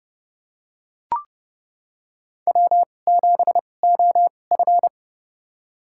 Morse code, heard as a keyed tone, about 30 wpm, spaced as for 22 wpm, 700 Hz. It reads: W7OF
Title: USA Call Signs - 4 Characters - 2x Character Spacing - 30wpm